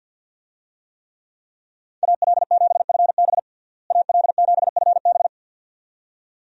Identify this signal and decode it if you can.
Morse code, heard as a keyed tone, about 52 wpm, 700 Hz. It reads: AL7FB AL7FB
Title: USA Call Signs - 5 Characters - Repeated 2x - 50wpm